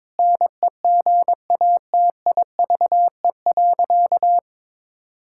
Morse code, heard as a keyed tone, about 22 wpm, 700 Hz. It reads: NEGATIVE.
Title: Sentences from Top 1000 Words - Review Part 6 - 22wpm